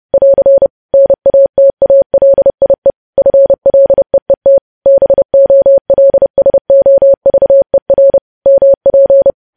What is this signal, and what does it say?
Morse code, heard as a keyed tone, about 30 wpm, 550 Hz. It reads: <AR> NATALIE FLEET BOLSOVER MP
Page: news headlines in morse code at 30 wpm